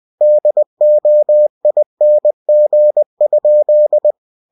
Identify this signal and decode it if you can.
Morse code, heard as a keyed tone, about 20 wpm, 600 Hz. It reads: DOING?
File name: cw-books / winnie-the-pooh / winnie-20wpm-page-0087